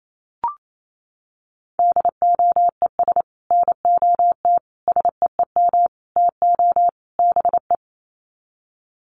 Morse code, heard as a keyed tone, about 28 wpm, 700 Hz. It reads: DOES NOT SEEM TO BE
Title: Sets of 5 Words 28wpm